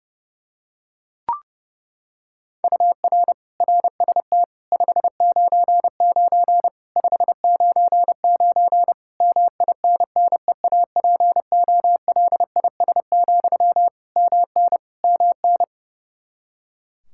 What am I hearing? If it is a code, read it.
Morse code, heard as a keyed tone, about 30 wpm, 700 Hz. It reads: UR RST 599 599 MINNEAPOLIS, MN MN
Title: Practice QSO - Exchange 1 - Beginning - 30wpm